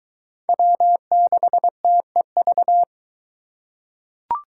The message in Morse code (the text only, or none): W6TEV